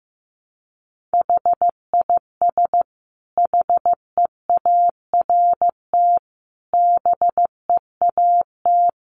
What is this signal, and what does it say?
Morse code, heard as a keyed tone, about 15 wpm, 700 Hz. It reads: HIS HEART BEAT
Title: Sentences from Top 1300 Words 15wpm